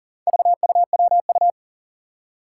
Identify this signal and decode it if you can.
Morse code, heard as a keyed tone, about 40 wpm, 700 Hz. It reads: VUWU
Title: Four Challenging Letter-Number Combinations 40wpm